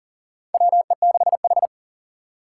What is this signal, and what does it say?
Morse code, heard as a keyed tone, about 40 wpm, 700 Hz. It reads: WE6H
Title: USA Call Signs - 4 Characters - 6x Character Spacing - 40wpm